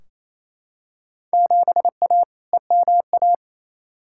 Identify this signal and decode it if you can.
Morse code, heard as a keyed tone, about 28 wpm, 700 Hz. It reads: 7A EMA
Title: Field Day Exchanges Encore - 28wpm